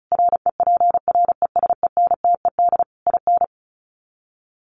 Morse code, heard as a keyed tone, about 34 wpm, 700 Hz. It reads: REPRESENTED IN
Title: Sets of 5 Words 35wpm